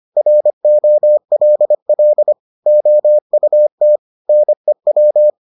Morse code, heard as a keyed tone, about 25 wpm, 600 Hz. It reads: ROLL OUT NEW